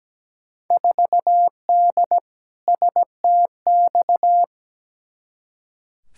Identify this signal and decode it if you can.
Morse code, heard as a keyed tone, about 17 wpm, 700 Hz. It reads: 4D STX
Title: Field Day Exchanges Encore - 17wpm